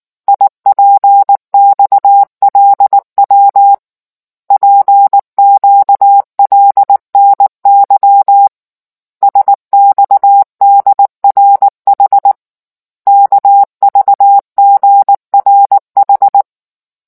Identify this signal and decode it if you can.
Morse code, heard as a keyed tone, about 19 wpm, 800 Hz. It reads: IPXLW PQLNY SXDRH KVGRH